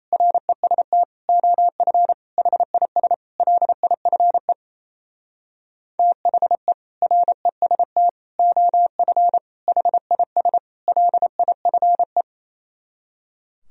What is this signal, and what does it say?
Morse code, heard as a keyed tone, about 36 wpm, 700 Hz. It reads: REST OF HIS LIFE THE REST OF HIS LIFE